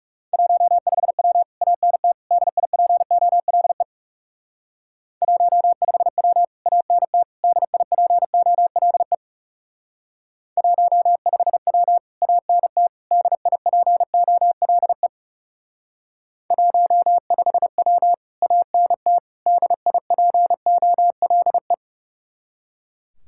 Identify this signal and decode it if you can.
Morse code, heard as a keyed tone, about 36 wpm, 700 Hz. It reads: 15W ANT DIPOLE 15W ANT DIPOLE 15W ANT DIPOLE 15W ANT DIPOLE
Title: Speed-Racing Common QSO Phrases Encore - 30wpm